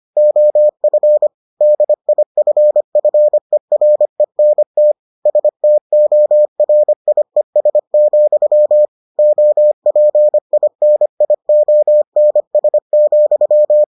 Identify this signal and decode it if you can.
Morse code, heard as a keyed tone, about 25 wpm, 600 Hz. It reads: OF DIFFERENT STORIES, OPINIONS,